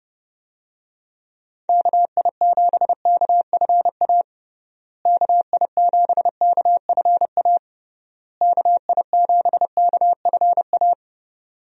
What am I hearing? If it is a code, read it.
Morse code, heard as a keyed tone, about 30 wpm, 700 Hz. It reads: KI7KFA KI7KFA KI7KFA
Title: USA Call Signs - 6 Characters - Repeated 3x - 30wpm